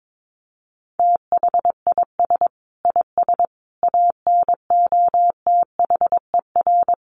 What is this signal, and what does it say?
Morse code, heard as a keyed tone, about 22 wpm, 700 Hz. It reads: THIS IS ANOTHER
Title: Sentences from Top 600 Words 22wpm